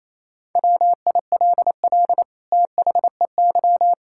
Morse code, heard as a keyed tone, about 28 wpm, 700 Hz. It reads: WILL THEY
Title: Sentences from Top 1600 Words 28wpm